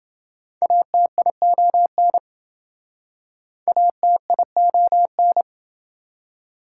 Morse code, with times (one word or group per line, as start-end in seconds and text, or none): 0.6-2.2 ATION
3.7-5.4 ATION